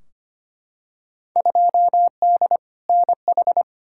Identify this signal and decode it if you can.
Morse code, heard as a keyed tone, about 25 wpm, 700 Hz. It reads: 2D NH
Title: Field Day Exchanges Encore - 25wpm